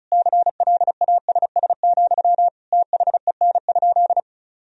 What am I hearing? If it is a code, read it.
Morse code, heard as a keyed tone, about 36 wpm, 700 Hz. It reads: CLASS, THEN?
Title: Sentences from Top 1000 Words - Review Part 4 - 35wpm